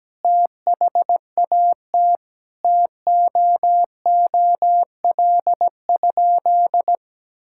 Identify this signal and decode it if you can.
Morse code, heard as a keyed tone, about 17 wpm, 700 Hz. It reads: THAT TOOL?
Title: Sentences from Top 1000 Words - Review Part 6 - 17wpm